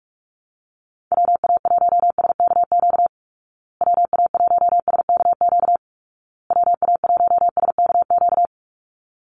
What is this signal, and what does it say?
Morse code, heard as a keyed tone, about 44 wpm, 700 Hz. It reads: WA1SKQ WA1SKQ WA1SKQ
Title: Call Signs Repeated 2x - 45wpm